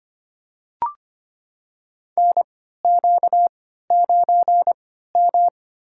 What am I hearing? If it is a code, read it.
Morse code, heard as a keyed tone, about 25 wpm, spaced as for 14 wpm, 700 Hz. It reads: NQ9M
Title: USA Call Signs - 4 Characters - 3x Character Spacing - 25wpm